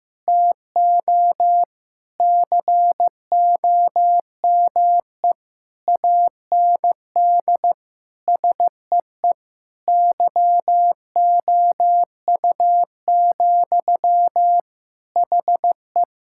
Morse code, heard as a keyed tone, about 15 wpm, 700 Hz. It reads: TO COME AND SEE YOU, HE
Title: Sentences from Top 1000 Words - Review Part 6 - 15wpm